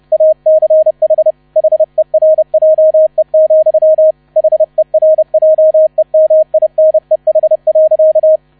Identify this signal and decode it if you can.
Morse code, heard as a keyed tone, about 30 wpm, 630 Hz, with radio noise. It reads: ACH HERJE, HERJEMINEH.